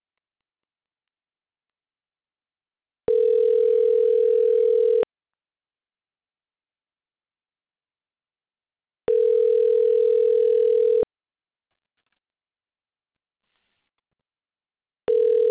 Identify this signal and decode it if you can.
Special information tone.